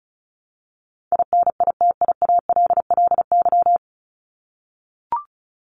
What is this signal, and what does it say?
Morse code, heard as a keyed tone, about 34 wpm, 700 Hz. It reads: INITIALLY
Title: Top 1251-1500 Words 35wpm